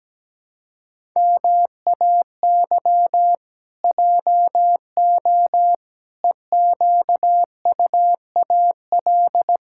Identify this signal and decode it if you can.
Morse code, heard as a keyed tone, about 17 wpm, 700 Hz. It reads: MAY JO EQUAL
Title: Pangram Challenge v2 - 17wpm